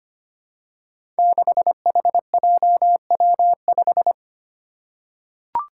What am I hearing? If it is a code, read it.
Morse code, heard as a keyed tone, about 25 wpm, 700 Hz. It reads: 6HJW5
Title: Five Challenging Letter-Number Combinations 25wpm